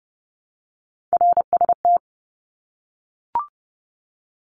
Morse code, heard as a keyed tone, about 30 wpm, 700 Hz. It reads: RST
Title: Abbreviations Repeated 3x - 30wpm